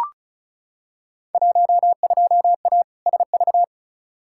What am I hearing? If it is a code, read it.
Morse code, heard as a keyed tone, about 36 wpm, 700 Hz. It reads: 12A SV
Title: Field Day Exchanges Encore - 35wpm